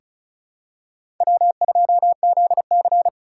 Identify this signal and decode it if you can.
Morse code, heard as a keyed tone, about 36 wpm, 700 Hz. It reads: W2ZC
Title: USA Call Signs - 4 Characters - 4x Character Spacing - 35wpm